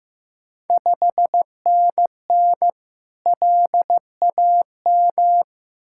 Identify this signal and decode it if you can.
Morse code, heard as a keyed tone, about 15 wpm, 700 Hz. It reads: 5NN LAM
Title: Ontario QSO Party Exchange 15wpm